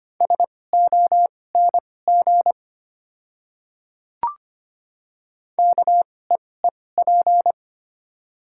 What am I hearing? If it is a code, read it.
Morse code, heard as a keyed tone, about 25 wpm, spaced as for 18 wpm, 700 Hz. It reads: SONG KEEP
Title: 2x Character Spacing - Top 500 Words - 25wpm